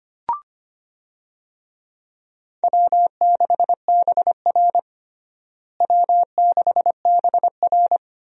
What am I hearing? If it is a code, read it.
Morse code, heard as a keyed tone, about 25 wpm, 700 Hz. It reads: W6BR W6BR
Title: USA Call Signs - 4 Characters - Repeated 2x - 25wpm